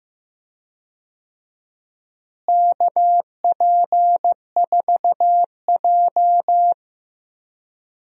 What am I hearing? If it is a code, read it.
Morse code, heard as a keyed tone, about 15 wpm, 700 Hz. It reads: KP4J
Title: DXCC-100 Call Signs - 4 Characters - Repeated 2x - 15wpm